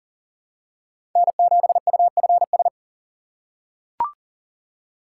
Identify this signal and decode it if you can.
Morse code, heard as a keyed tone, about 40 wpm, 700 Hz. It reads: N7UFS
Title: USA Call Signs - 5 Characters - 3x Character Spacing - 40wpm